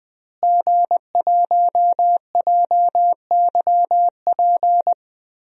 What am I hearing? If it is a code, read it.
Morse code, heard as a keyed tone, about 21 wpm, 700 Hz. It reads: G1JYP
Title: DXCC-100 Call Signs - 5 Characters - Repeated 3x - 20wpm